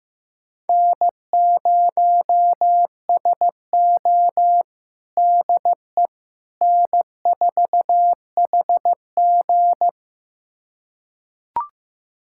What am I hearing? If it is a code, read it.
Morse code, heard as a keyed tone, about 16 wpm, 700 Hz. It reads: N0SO DE N4HG
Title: USA Callsign DE Callsign - 4 Characters - 15wpm